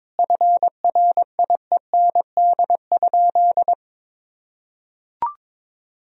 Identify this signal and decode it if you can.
Morse code, heard as a keyed tone, about 22 wpm, 700 Hz. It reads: FRIEND?
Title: Sentences from Top 300 Words and Abbreviations 22wpm